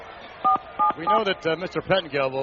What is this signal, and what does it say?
DTMF, heard as telephone touch tones, tones of about 120 ms, with speech. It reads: 477